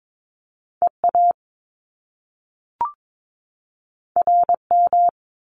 Morse code, heard as a keyed tone, about 22 wpm, 700 Hz. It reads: EA RM
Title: Top Two Letter Combinations in Words 22wpm